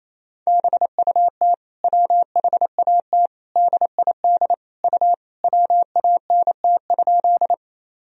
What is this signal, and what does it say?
Morse code, heard as a keyed tone, about 28 wpm, 700 Hz. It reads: BUT WHAT DID U WANT?
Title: Sentences from Top 200 Words and Abbreviations 28wpm